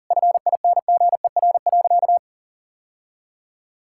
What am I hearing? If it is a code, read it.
Morse code, heard as a keyed tone, about 40 wpm, 700 Hz. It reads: FINGER.
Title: Sentences from Top 1600 Words 40wpm